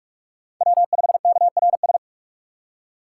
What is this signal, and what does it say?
Morse code, heard as a keyed tone, about 46 wpm, 700 Hz. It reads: W5KRS
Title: USA Call Signs - 5 Characters - 3x Character Spacing - 45wpm